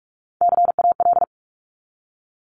Morse code, heard as a keyed tone, about 46 wpm, 700 Hz. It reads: CAL
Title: Top Three Letter Combinations in Words 45wpm